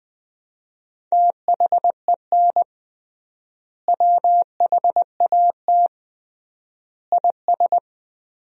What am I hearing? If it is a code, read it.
Morse code, heard as a keyed tone, about 20 wpm, 700 Hz. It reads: THEN WHAT IS
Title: 3x Word-Spacing - Sentences from Top 100 Words - 20wpm